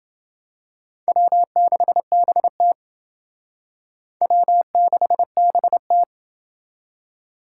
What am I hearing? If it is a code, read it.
Morse code, heard as a keyed tone, about 29 wpm, 700 Hz. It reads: W6BT W6BT